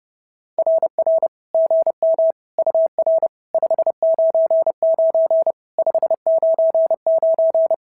Morse code, heard as a keyed tone, about 30 wpm, 650 Hz. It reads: RR GM UR 599 599
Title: SOTA QSO Practice 30wpm